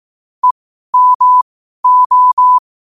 Morse code, heard as a keyed tone, about 18 wpm, spaced as for 12 wpm, 1000 Hz. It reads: EMO
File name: qotdfiles / 12wpm